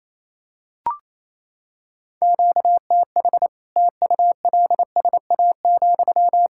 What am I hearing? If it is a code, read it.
Morse code, heard as a keyed tone, about 27 wpm, 700 Hz. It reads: QTH TULSA,
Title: Practice QSO - Exchange 1 - Beginning - 28wpm